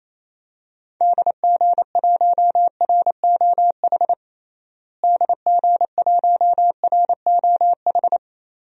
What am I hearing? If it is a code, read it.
Morse code, heard as a keyed tone, about 27 wpm, 700 Hz. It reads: DG1ROH DG1ROH
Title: DXCC-100 Call Signs - 6 Characters - Repeated 2x - 28wpm